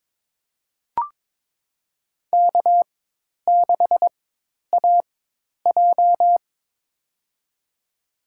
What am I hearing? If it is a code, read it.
Morse code, heard as a keyed tone, about 22 wpm, spaced as for 10 wpm, 700 Hz. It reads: K6AJ
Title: USA Call Signs - 4 Characters - 4x Character Spacing - 22wpm